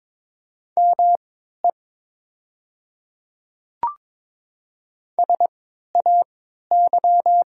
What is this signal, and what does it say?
Morse code, heard as a keyed tone, about 22 wpm, spaced as for 12 wpm, 690 Hz. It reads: ME SAY
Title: 3x Character Spacing - Top 300 Words - 22wpm